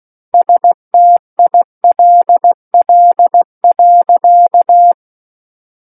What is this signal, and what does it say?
Morse code, heard as a keyed tone, about 16 wpm, 700 Hz. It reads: STILL.
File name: Book One / 07-Chapter 7 - 16 WPM